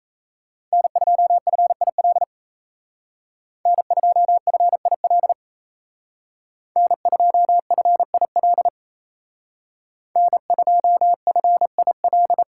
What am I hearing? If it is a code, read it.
Morse code, heard as a keyed tone, about 36 wpm, 700 Hz. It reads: N2FIL N2FIL N2FIL N2FIL